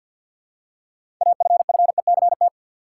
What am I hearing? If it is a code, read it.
Morse code, heard as a keyed tone, about 51 wpm, 700 Hz. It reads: AFFECT
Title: Six Letter Words - 50wpm